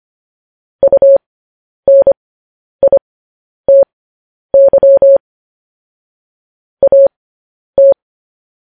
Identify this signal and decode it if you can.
Morse code, heard as a keyed tone, about 25 wpm, spaced as for 10 wpm, 550 Hz. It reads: UNITY AT